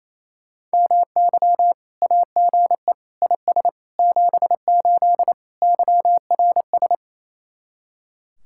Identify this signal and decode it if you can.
Morse code, heard as a keyed tone, about 36 wpm, 700 Hz. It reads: MY AGE IS 78 YRS